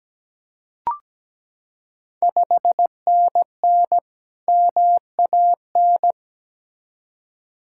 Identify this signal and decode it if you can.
Morse code, heard as a keyed tone, about 17 wpm, 700 Hz. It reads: 5NN MAN
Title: Ontario QSO Party Exchange 17wpm